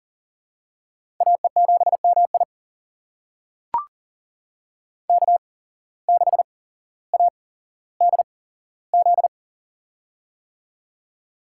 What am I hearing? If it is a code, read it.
Morse code, heard as a keyed tone, about 40 wpm, spaced as for 17 wpm, 700 Hz. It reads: AE7MI K6ADZ
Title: USA Call Signs - 5 Characters - 8x Character Spacing - 40wpm